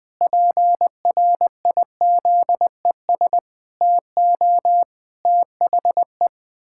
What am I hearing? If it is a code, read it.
Morse code, heard as a keyed tone, about 20 wpm, 700 Hz. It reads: PRIZES TO THE
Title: Pangram Challenge v2 - 20wpm